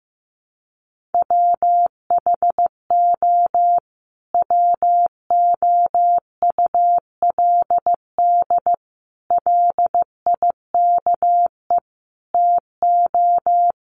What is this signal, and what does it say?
Morse code, heard as a keyed tone, about 15 wpm, 700 Hz. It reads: WHO WOULD LIKE TO
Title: Sentences from Top 1500 Words 15wpm